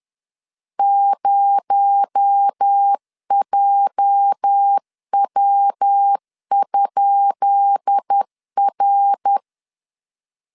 Morse code, heard as a keyed tone, about 10 wpm, 800 Hz. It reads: 0JW?R
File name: LEZIONI / 10 SIGNS / 10B INT 10wpm 2844